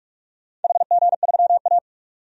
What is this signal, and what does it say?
Morse code, heard as a keyed tone, about 44 wpm, 700 Hz. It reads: HG3A
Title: DXCC-100 Call Signs - 4 Characters - 45wpm